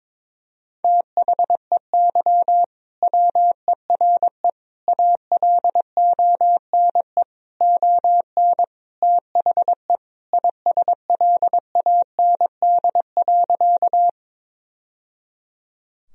Morse code, heard as a keyed tone, about 22 wpm, 700 Hz. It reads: THEY WERE ALONE ON THE ISLAND.
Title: Sentences from Top 1000 Words - Review Part 3 - 22wpm